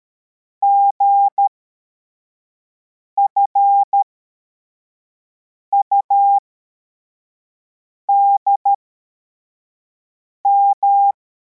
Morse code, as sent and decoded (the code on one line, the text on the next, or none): --. ..-. ..- -.. --
GFUDM